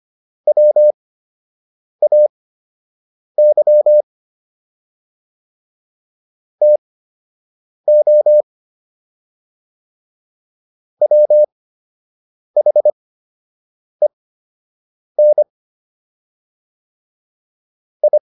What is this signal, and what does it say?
Morse code, heard as a keyed tone, about 25 wpm, spaced as for 7 wpm, 600 Hz. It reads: WAY TO WHEN I